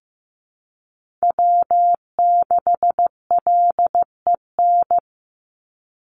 Morse code, heard as a keyed tone, about 15 wpm, 700 Hz. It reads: W6LEN
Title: Call Signs Repeated 3x - 15wpm